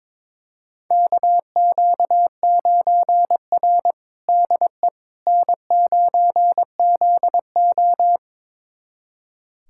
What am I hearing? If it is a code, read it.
Morse code, heard as a keyed tone, about 22 wpm, 700 Hz. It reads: KQ9R DE N9ZO
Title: USA Callsign DE Callsign - 4 Characters - 22wpm